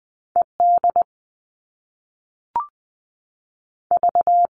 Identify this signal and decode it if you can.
Morse code, heard as a keyed tone, about 20 wpm, spaced as for 5 wpm, 700 Hz. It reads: EDV
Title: All Cut Numbers - 2-Digit 20wpm